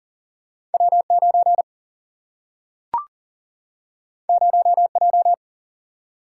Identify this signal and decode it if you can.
Morse code, heard as a keyed tone, about 39 wpm, 700 Hz. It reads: W9 0J
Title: Single-Letter Single-Number 40wpm